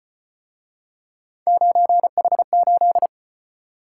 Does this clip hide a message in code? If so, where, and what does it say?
Yes: Morse code from 1.5 to 3.1 s, heard as a keyed tone, about 34 wpm, 700 Hz. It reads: 9H8